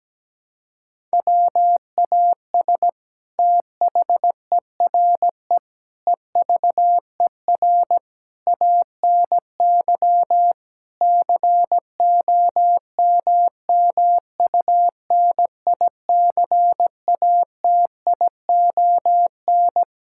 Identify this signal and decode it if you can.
Morse code, heard as a keyed tone, about 17 wpm, 700 Hz. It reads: WAS THERE EVER ANY COMMUNICATION
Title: Sentences from Top 1000 Words - Review Part 6 - 17wpm